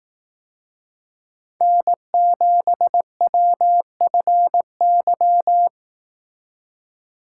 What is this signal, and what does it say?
Morse code, heard as a keyed tone, about 18 wpm, 700 Hz. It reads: N7WFY
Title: Speed Racing - USA 1x3 Call Signs 15wpm